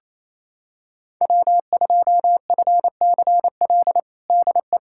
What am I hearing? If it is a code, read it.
Morse code, heard as a keyed tone, about 28 wpm, 700 Hz. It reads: W2FCL DE
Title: USA Callsign DE Callsign 28wpm